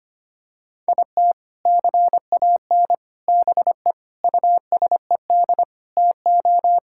Morse code, heard as a keyed tone, about 25 wpm, 700 Hz. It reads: IT CAN BE USED TO
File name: practice / 25wpm / Sentences with 4-Word Phrases - 25wpm